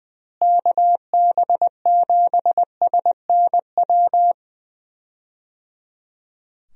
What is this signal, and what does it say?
Morse code, heard as a keyed tone, about 20 wpm, 700 Hz. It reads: KB7SNW
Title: USA Call Signs - 6 Characters - Repeated 3x - 20wpm